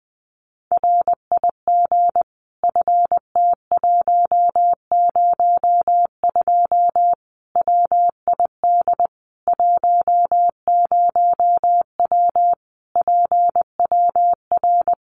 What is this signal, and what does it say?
Morse code, heard as a keyed tone, about 20 wpm, 700 Hz. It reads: RIG FT102 WID 10W PWR
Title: Common QSO Phrases 20wpm